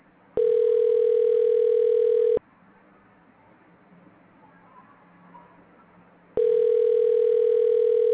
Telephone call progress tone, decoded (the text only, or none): ringback tone